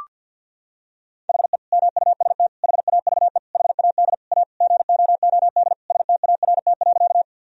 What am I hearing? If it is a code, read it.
Morse code, heard as a keyed tone, about 51 wpm, 700 Hz. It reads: HE MUST HAVE HAD A GOOD START.
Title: Sentences from Top 1000 Words - Review Part 1 - 50wpm